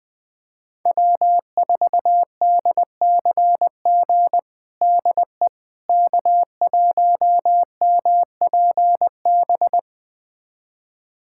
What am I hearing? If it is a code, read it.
Morse code, heard as a keyed tone, about 21 wpm, 700 Hz. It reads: W4DCG DE K1MPB
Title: USA Callsign DE Callsign - 5 Characters - 20wpm